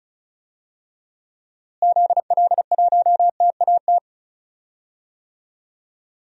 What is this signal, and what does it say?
Morse code, heard as a keyed tone, about 36 wpm, 700 Hz. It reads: ZL1TAT
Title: DXCC-100 Call Signs - 6 Characters - Repeated 3x - 35wpm